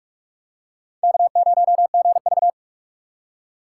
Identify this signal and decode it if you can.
Morse code, heard as a keyed tone, about 46 wpm, 700 Hz. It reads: K0GV